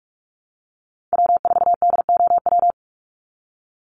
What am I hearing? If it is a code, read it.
Morse code, heard as a keyed tone, about 44 wpm, 700 Hz. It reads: W4DOW
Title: NA SOTA Chaser Callsigns 45wpm